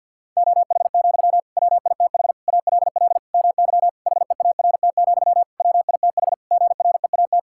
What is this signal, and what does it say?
Morse code, heard as a keyed tone, about 52 wpm, 700 Hz. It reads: OH, WITH ALL MY HEART, WITH GREAT
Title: Sentences from Top 1600 Words 50wpm